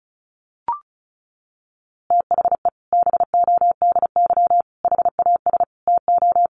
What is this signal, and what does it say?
Morse code, heard as a keyed tone, about 34 wpm, 700 Hz. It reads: THE BODY HAS TO
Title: A Sound Of Thunder 35wpm